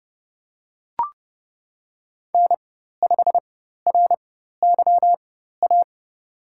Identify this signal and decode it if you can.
Morse code, heard as a keyed tone, about 30 wpm, spaced as for 14 wpm, 700 Hz. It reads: N5RYA
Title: USA Call Signs - 5 Characters - 4x Character Spacing - 30wpm